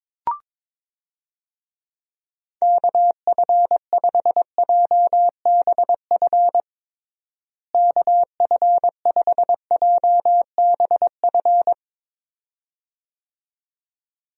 Morse code, heard as a keyed tone, about 22 wpm, 700 Hz. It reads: KF5JBF KF5JBF
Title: USA Call Signs - 6 Characters - Repeated 2x - 22wpm